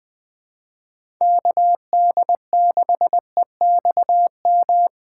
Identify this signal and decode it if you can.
Morse code, heard as a keyed tone, about 20 wpm, 700 Hz. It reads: KD6EXM
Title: USA Call Signs - 6 Characters - 8x Character Spacing - 20wpm